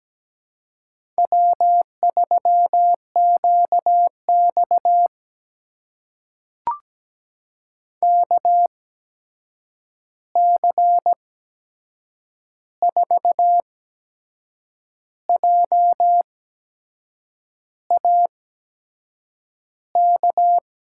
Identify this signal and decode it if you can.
Morse code, heard as a keyed tone, about 17 wpm, spaced as for 6 wpm, 700 Hz. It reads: W3QX KC4JAK
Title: USA Call Signs - 8x Character Spacing - 17wpm